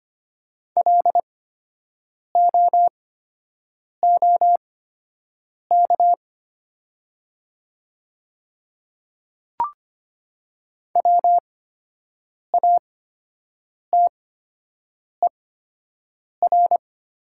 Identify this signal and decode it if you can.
Morse code, heard as a keyed tone, about 25 wpm, spaced as for 7 wpm, 700 Hz. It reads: LOOK WATER
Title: 8x Character Spacing - Top 300 Words - 25wpm